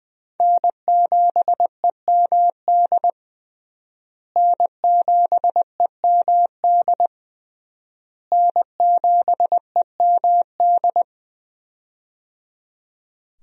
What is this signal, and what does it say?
Morse code, heard as a keyed tone, about 21 wpm, 700 Hz. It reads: N7EMD N7EMD N7EMD
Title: USA Call Signs - 5 Characters - Repeated 3x - 20wpm